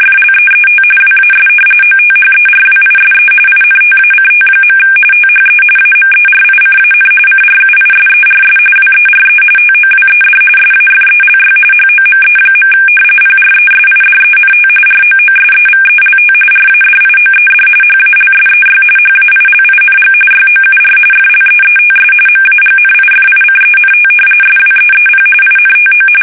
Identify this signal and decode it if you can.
RTTY, about 45.45 baud. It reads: FLEET WAS OTHERWISE INTACT. THE AMERICAN CRUISERS WERE TOWED SAFELY HOME. PRESIDENT ROOSEVELT RADIOED HALSEY HAILING YOUR FLEETS MAGNIFICENT SWEEP INTO
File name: fsk850 / battleshipnj